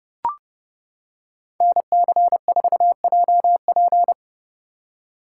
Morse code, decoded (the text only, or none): NC4JP